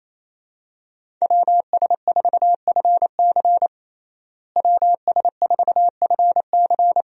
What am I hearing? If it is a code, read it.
Morse code, heard as a keyed tone, about 28 wpm, 700 Hz. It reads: WS4FC WS4FC
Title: USA Call Signs - 5 Characters - Repeated 3x - 28wpm